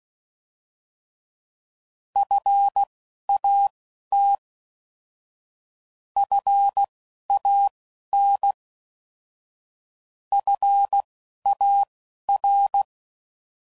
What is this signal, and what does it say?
Morse code, heard as a keyed tone, about 16 wpm, spaced as for 11 wpm, 800 Hz. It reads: FAT FAN FAR
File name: Prefixes / By First Letter / Fs